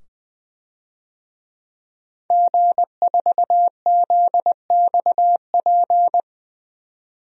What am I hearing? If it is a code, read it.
Morse code, heard as a keyed tone, about 20 wpm, 700 Hz. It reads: G4ZXP
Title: DXCC-100 Call Signs - 5 Characters - 20wpm